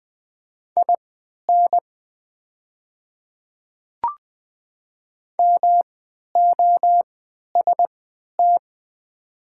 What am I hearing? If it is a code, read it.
Morse code, heard as a keyed tone, about 20 wpm, spaced as for 11 wpm, 700 Hz. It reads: IN MOST
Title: 3x Character Spacing - Top 100 Words - 20wpm